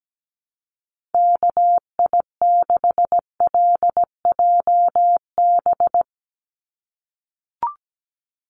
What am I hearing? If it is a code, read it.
Morse code, heard as a keyed tone, about 17 wpm, 700 Hz. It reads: KI6LJB
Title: USA Call Signs 17wpm